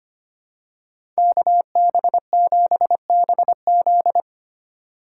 Morse code, heard as a keyed tone, about 29 wpm, 700 Hz. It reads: KB7BZ
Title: Speed Racing - USA 2x2 Call Signs 25wpm